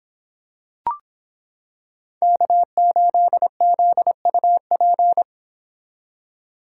Morse code, heard as a keyed tone, about 26 wpm, 700 Hz. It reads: K8ZUP